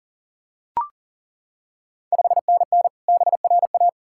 Morse code, heard as a keyed tone, about 38 wpm, 690 Hz. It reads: 5NN BRA
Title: Ontario QSO Party Exchange 40wpm